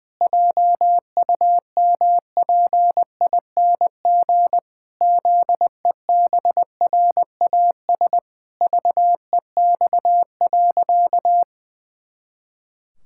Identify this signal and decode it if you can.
Morse code, heard as a keyed tone, about 20 wpm, 700 Hz. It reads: JUMPING ZEBRAS VEX.
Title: Pangram Challenge v2 - 20wpm